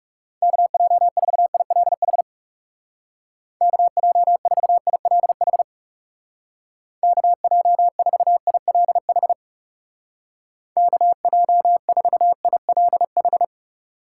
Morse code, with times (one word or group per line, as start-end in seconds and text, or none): 0.4-2.2 KJ4ILH
3.6-5.6 KJ4ILH
7.0-9.3 KJ4ILH
10.8-13.4 KJ4ILH